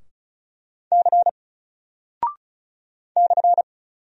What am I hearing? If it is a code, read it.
Morse code, heard as a keyed tone, about 36 wpm, 700 Hz. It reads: C /
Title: Single Character (Letter:Number:Punctuation:Prosign) 35wpm